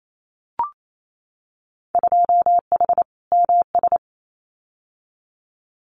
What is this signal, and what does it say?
Morse code, heard as a keyed tone, about 28 wpm, 700 Hz. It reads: 2H MS
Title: Winter Field Day Exchanges 28wpm